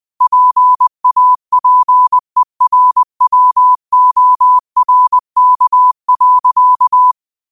Morse code, heard as a keyed tone, about 20 wpm, 1000 Hz. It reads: PAPERWORK.